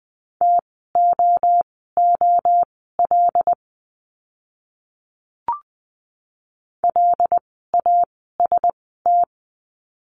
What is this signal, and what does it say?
Morse code, heard as a keyed tone, about 20 wpm, spaced as for 14 wpm, 700 Hz. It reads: TOOL LAST
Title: 075 Lesson 13f - Letter L - Words 20wpm